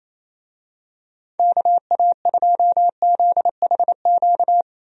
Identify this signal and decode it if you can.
Morse code, heard as a keyed tone, about 27 wpm, 700 Hz. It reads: KA2ZHQ
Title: USA Call Signs - 6 Characters - 28wpm